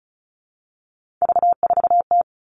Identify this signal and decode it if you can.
Morse code, heard as a keyed tone, about 34 wpm, 700 Hz. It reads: V4T
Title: All Cut Numbers - 3-Digit 35wpm